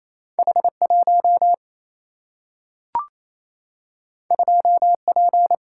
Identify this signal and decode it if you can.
Morse code, heard as a keyed tone, about 28 wpm, 700 Hz. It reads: H 1 2 P
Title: Single-Letter Single-Number 28wpm